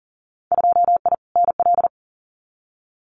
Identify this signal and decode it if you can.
Morse code, heard as a keyed tone, about 40 wpm, 700 Hz. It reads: 2I NL
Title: Winter Field Day Exchanges Encore 40wpm